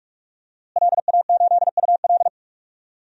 Morse code, heard as a keyed tone, about 42 wpm, 700 Hz. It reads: LA8UL
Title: DXCC-100 Call Signs - 5 Characters - Repeated 2x - 45wpm